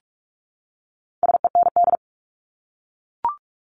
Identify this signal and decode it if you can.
Morse code, heard as a keyed tone, about 44 wpm, 700 Hz. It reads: SEND
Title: Top 1251-1500 Words 45wpm